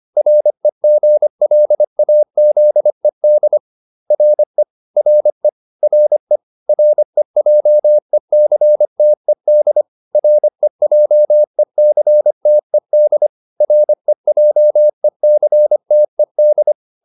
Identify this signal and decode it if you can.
Morse code, heard as a keyed tone, about 25 wpm, 600 Hz. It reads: REGLAZED RE RE RE REJECTED REJECTED REJECTED